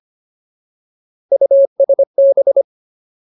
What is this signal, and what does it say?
Morse code, heard as a keyed tone, about 25 wpm, 550 Hz. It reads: USB
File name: Ultimate Word List / 25WPM Ultimate